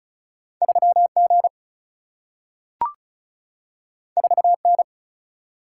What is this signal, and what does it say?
Morse code, heard as a keyed tone, about 36 wpm, 700 Hz. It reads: 3 G 4 N
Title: Single-Letter Single-Number 35wpm